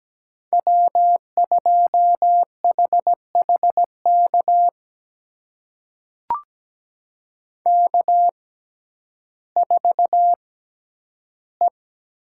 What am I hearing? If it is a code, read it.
Morse code, heard as a keyed tone, about 17 wpm, spaced as for 10 wpm, 700 Hz. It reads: W2HHK K4E
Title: USA Call Signs - 5 Characters - 6x Character Spacing - 17wpm